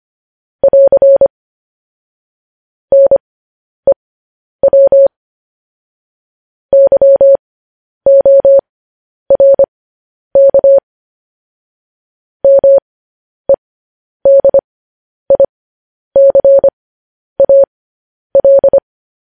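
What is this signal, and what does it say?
Morse code, heard as a keyed tone, about 25 wpm, spaced as for 10 wpm, 550 Hz. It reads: <AR> NEW YORK MEDICAL
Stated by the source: Morse news headlines sent at 10 wpm.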